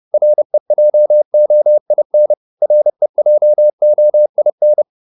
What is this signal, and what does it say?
Morse code, heard as a keyed tone, about 30 wpm, 600 Hz. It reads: REJOIN REJOIN